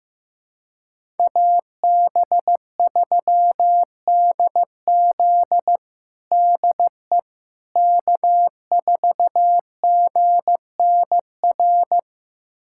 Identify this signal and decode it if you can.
Morse code, heard as a keyed tone, about 15 wpm, 700 Hz. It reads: AB3DZ DE K4GNR
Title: USA Callsign DE Callsign - 5 Characters - 15wpm